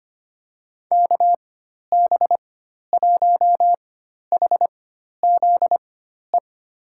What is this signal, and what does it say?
Morse code, heard as a keyed tone, about 25 wpm, spaced as for 12 wpm, 700 Hz. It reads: KB1HZE
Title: USA Call Signs - 6 Characters - 4x Character Spacing - 25wpm